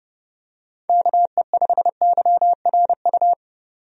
Morse code, heard as a keyed tone, about 32 wpm, 700 Hz. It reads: KE5YRU
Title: USA Call Signs - 6 Characters - 30wpm